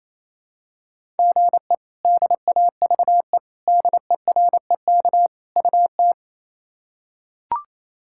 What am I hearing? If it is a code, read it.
Morse code, heard as a keyed tone, about 28 wpm, 700 Hz. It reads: GE DAVE DEREK UT
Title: K1USN SST QSO Practice - 28wpm